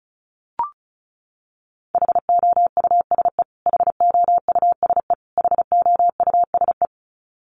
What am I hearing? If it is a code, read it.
Morse code, heard as a keyed tone, about 34 wpm, 700 Hz. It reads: HOUSE HOUSE HOUSE
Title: Top 500 Words Repeated 35wpm